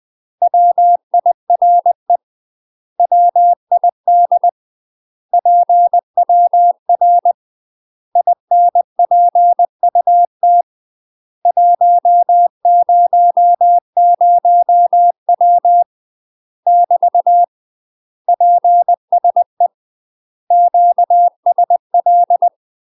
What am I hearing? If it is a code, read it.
Morse code, heard as a keyed tone, about 20 wpm, 700 Hz. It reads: WIRE WID PWR INPUT 100W <BT> PSE QSL
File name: UK MASTER QSOs TRAINING / 20WPM / UK MASTER QSOs 0055